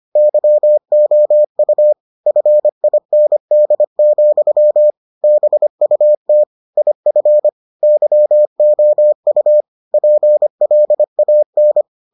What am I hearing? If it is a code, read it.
Morse code, heard as a keyed tone, about 25 wpm, 600 Hz. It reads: YOU FIND, BUT IF YOU PLAN